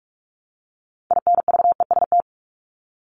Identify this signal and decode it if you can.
Morse code, heard as a keyed tone, about 44 wpm, 700 Hz. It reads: INVEST
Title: Top 1001-1250 Words 45wpm